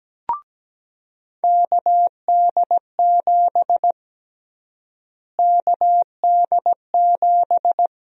Morse code, heard as a keyed tone, about 17 wpm, 700 Hz. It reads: KD7 KD7
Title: USA Call Sign Prefixes - Repeated 2x - 17wpm